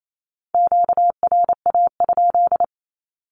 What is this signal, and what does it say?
Morse code, heard as a keyed tone, about 28 wpm, 700 Hz. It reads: QRA?
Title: Q-Codes 28wpm